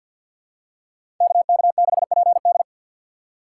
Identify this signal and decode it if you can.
Morse code, heard as a keyed tone, about 52 wpm, 700 Hz. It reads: KK6PD
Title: USA Call Signs - 5 Characters - 4x Character Spacing - 50wpm